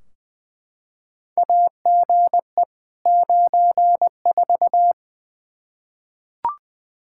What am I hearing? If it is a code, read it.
Morse code, heard as a keyed tone, about 20 wpm, 700 Hz. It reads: AGE 94
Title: K1USN SST QSO Practice 20wpm